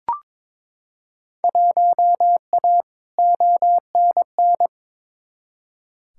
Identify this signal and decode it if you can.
Morse code, heard as a keyed tone, about 22 wpm, 700 Hz. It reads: 1A ONN
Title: Field Day Exchanges Encore - 22wpm